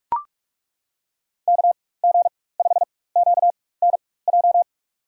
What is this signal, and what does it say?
Morse code, heard as a keyed tone, about 42 wpm, spaced as for 20 wpm, 690 Hz. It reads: KG5QNJ
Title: USA Call Signs - 4x Character Spacing - 45wpm